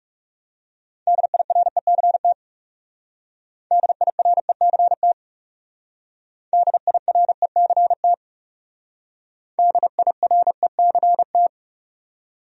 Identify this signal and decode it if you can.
Morse code, heard as a keyed tone, about 36 wpm, 700 Hz. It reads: DIRECT DIRECT DIRECT DIRECT
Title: Speed-Racing Top 500 Words 30wpm